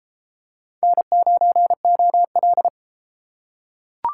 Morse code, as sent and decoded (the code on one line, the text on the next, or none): -. ----. --- .-..
N9OL